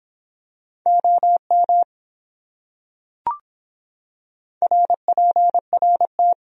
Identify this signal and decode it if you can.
Morse code, heard as a keyed tone, about 25 wpm, 700 Hz. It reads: OM RPRT